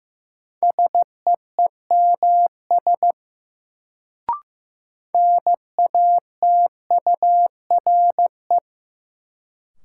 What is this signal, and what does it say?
Morse code, heard as a keyed tone, about 15 wpm, 700 Hz. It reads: SEEMS NATURE
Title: Sets of 3 Words - New Word Drill - 15wpm